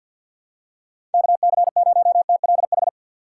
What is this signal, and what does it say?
Morse code, heard as a keyed tone, about 51 wpm, 700 Hz. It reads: KC0TLH
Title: USA Call Signs - 6 Characters - 4x Character Spacing - 50wpm